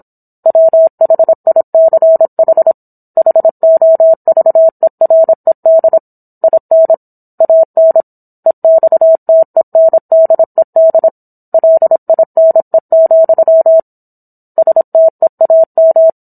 Morse code, heard as a keyed tone, about 27 wpm, 650 Hz. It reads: WHICH HOVERED IN AN EXTENDED LINE, STEAM